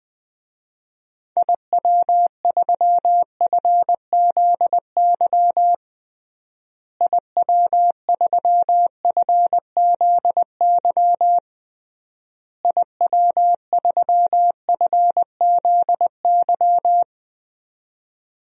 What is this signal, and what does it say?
Morse code, heard as a keyed tone, about 20 wpm, 700 Hz. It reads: IW3FZY IW3FZY IW3FZY
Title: DXCC-100 Call Signs - 6 Characters - Repeated 3x - 20wpm